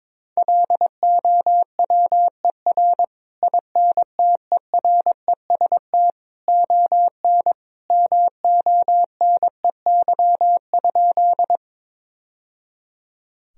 Morse code, heard as a keyed tone, about 21 wpm, 700 Hz. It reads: LOWER INTEREST ON MONEY?
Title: Sentences from Top 1000 Words - Review Part 4 - 22wpm